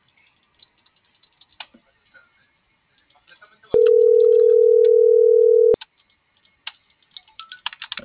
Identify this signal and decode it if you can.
Ringback tone (North America), 440 + 480 Hz.